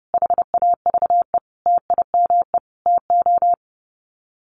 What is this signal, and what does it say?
Morse code, heard as a keyed tone, about 30 wpm, 700 Hz. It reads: HAVE TIME TO
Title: Sets of 5 Words Encore 30wpm